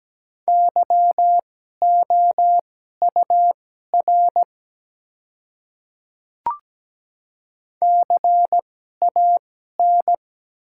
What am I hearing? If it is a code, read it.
Morse code, heard as a keyed tone, about 17 wpm, spaced as for 12 wpm, 700 Hz. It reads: YOUR CAN
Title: 2x Character Spacing - Top 100 Words - 17wpm